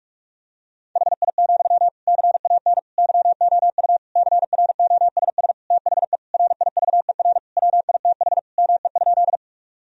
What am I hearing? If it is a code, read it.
Morse code, heard as a keyed tone, about 41 wpm, 700 Hz. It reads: HI, CAN YOU CROSS THE RIVER WITH ME?
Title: Sentences from Top 1000 Words - Review Part 3 - 45wpm